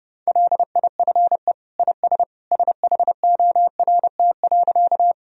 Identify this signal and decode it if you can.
Morse code, heard as a keyed tone, about 32 wpm, 700 Hz. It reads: LIFE IS SHORT.
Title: Sentences from Top 1000 Words - Review Part 2 - 30wpm